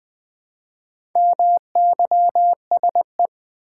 Morse code, heard as a keyed tone, about 20 wpm, 700 Hz. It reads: MYSE
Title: Build A Word - 2 Letters At A Time - 6-Letter Words - 20wpm